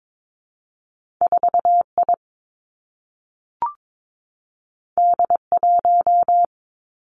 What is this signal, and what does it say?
Morse code, heard as a keyed tone, about 22 wpm, 700 Hz. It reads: 4 I D 1
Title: Two Challenging Letter-Number Combinations 22wpm